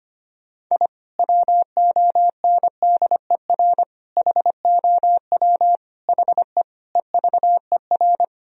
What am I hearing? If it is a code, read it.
Morse code, heard as a keyed tone, about 25 wpm, 700 Hz. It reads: I WONDER HOW HE EVER
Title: Sentences from Top 1000 Words - Review Part 3 - 25wpm